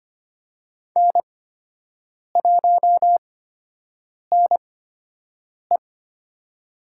Morse code, heard as a keyed tone, about 25 wpm, spaced as for 7 wpm, 700 Hz. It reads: N1NE
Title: USA Call Signs - 4 Characters - 8x Character Spacing - 25wpm